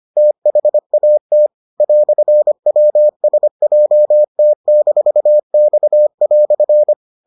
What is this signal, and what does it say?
Morse code, heard as a keyed tone, about 25 wpm, 600 Hz. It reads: THAT "WSJT-X"